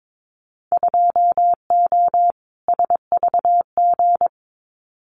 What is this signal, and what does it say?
Morse code, heard as a keyed tone, about 22 wpm, 700 Hz. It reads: 2O SVG
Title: Winter Field Day Exchanges 22wpm